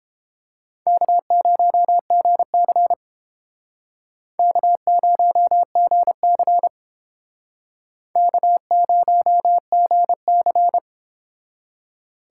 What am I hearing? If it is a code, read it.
Morse code, heard as a keyed tone, about 30 wpm, 700 Hz. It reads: K0GC K0GC K0GC